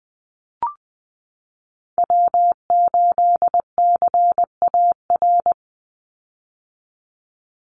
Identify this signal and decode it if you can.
Morse code, heard as a keyed tone, about 19 wpm, 700 Hz. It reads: W8CAR
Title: Call Signs - 20wpm